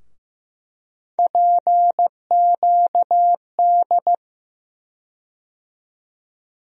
Morse code, heard as a keyed tone, about 15 wpm, 700 Hz. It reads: PQD
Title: USA Call Sign Postfixes - 3 Letters 15wpm